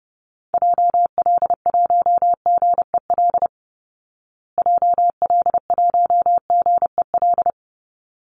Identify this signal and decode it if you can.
Morse code, heard as a keyed tone, about 30 wpm, 700 Hz. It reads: JL1GEL JL1GEL
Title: Call Signs Repeated 3x - 30wpm